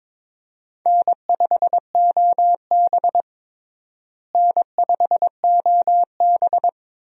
Morse code, heard as a keyed tone, about 21 wpm, 700 Hz. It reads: N5OB N5OB
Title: USA Call Signs - 4 Characters - Repeated 2x - 22wpm